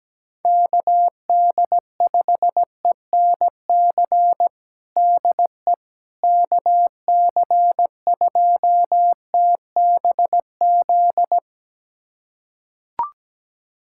Morse code, heard as a keyed tone, about 17 wpm, 700 Hz. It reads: KD5ENC DE KC2TBZ
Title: USA Callsign DE Callsign - 6 Characters - 17wpm